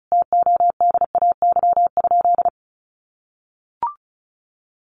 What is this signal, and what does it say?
Morse code, heard as a keyed tone, about 34 wpm, 700 Hz. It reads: TODAY?
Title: Sentences from Top 1200 Words 35wpm